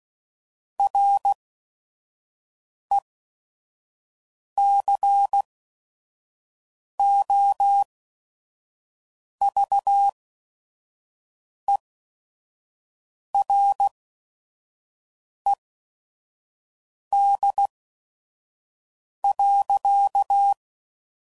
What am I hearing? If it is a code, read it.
Morse code, heard as a keyed tone, about 16 wpm, spaced as for 5 wpm, 800 Hz. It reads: RECOVERED.